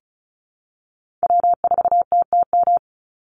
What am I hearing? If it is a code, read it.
Morse code, heard as a keyed tone, about 34 wpm, 700 Hz. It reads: W4TTM
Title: Call Signs Repeated 3x - 35wpm